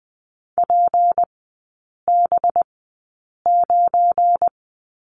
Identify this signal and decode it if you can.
Morse code, heard as a keyed tone, about 20 wpm, spaced as for 8 wpm, 700 Hz. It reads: PB9
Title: 184 Lesson 30e - Letter b - Three Characters 20wpm